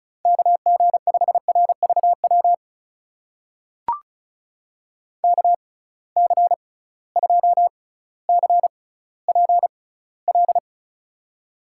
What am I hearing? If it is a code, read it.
Morse code, heard as a keyed tone, about 36 wpm, spaced as for 18 wpm, 700 Hz. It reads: KG5RVW KC2CPL
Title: USA Call Signs - 6 Characters - 6x Character Spacing - 35wpm